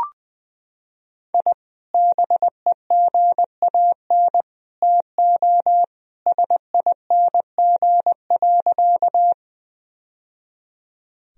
Morse code, heard as a keyed tone, about 20 wpm, 700 Hz. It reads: I BEGAN TO SING.
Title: Sentences from Top 1000 Words - Review Part 2 - 20wpm